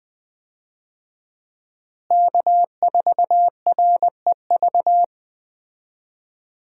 Morse code, heard as a keyed tone, about 20 wpm, 700 Hz. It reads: K4REV
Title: USA Call Signs - 5 Characters - Repeated 2x - 20wpm